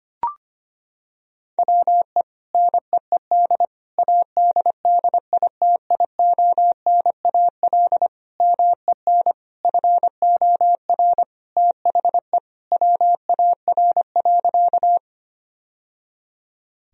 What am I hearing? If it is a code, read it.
Morse code, heard as a keyed tone, about 25 wpm, 700 Hz. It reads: WE NEED ADDITIONAL MEN FOR THE WAR.
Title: Sentences from Top 1000 Words - Review Part 4 - 25wpm